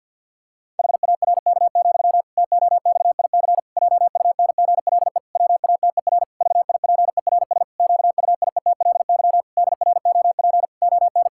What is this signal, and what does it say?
Morse code, heard as a keyed tone, about 50 wpm, 700 Hz. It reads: HARK, TOXIC JUNGLE WATER VIPERS QUIETLY DROP ON
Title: Pangram Challenge v2 - 50wpm